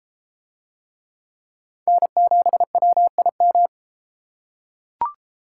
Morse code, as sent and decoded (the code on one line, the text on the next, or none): -. --... .-- .. --
N7WIM